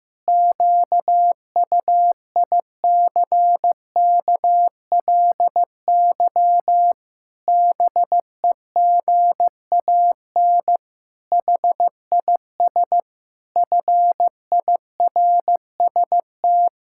Morse code, heard as a keyed tone, about 15 wpm, 700 Hz. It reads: QUICKLY BEGAN HIS FIRST
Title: Pangram Challenge v2 - 15wpm